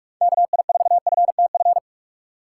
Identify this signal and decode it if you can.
Morse code, heard as a keyed tone, about 46 wpm, 700 Hz. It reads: KI4FTF
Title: USA Call Signs - 2x Character Spacing - 45wpm